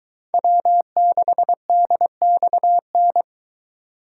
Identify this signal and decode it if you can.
Morse code, heard as a keyed tone, about 23 wpm, 700 Hz. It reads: W6DXN